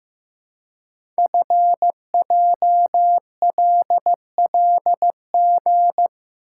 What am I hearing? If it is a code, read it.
Morse code, heard as a keyed tone, about 15 wpm, 700 Hz. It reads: FJLLG
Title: Code Groups - 5 Letters - 15wpm